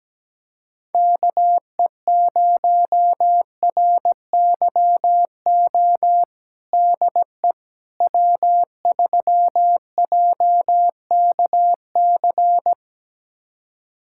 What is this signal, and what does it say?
Morse code, heard as a keyed tone, about 17 wpm, 700 Hz. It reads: KE0RYO DE W3JKC
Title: USA Callsign DE Callsign 17wpm